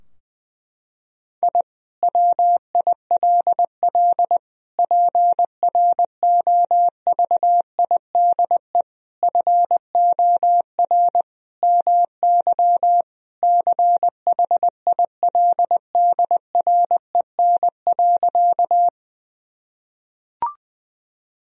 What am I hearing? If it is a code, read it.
Morse code, heard as a keyed tone, about 20 wpm, 700 Hz. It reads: I WILL PROVIDE FOR MY CHILDREN.
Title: Sentences from Top 1000 Words - Review Part 3 - 20wpm